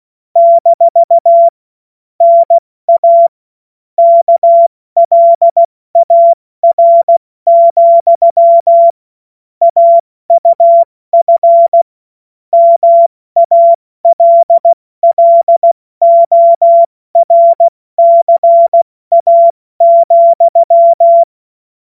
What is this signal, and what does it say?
Morse code, heard as a keyed tone, about 16 wpm, 680 Hz. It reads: - NA KLAR, AUF MALLORCA,
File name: Morsekurs / Witze / Witz 29. November 2022 70 F80